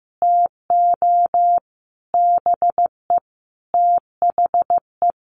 Morse code, heard as a keyed tone, about 15 wpm, 700 Hz. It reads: TO BE THE
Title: Sets of 4 Words 15wpm